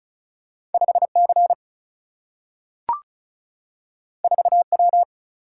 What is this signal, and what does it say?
Morse code, heard as a keyed tone, about 36 wpm, 700 Hz. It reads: 5 C 4 W
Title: Single-Letter Single-Number 35wpm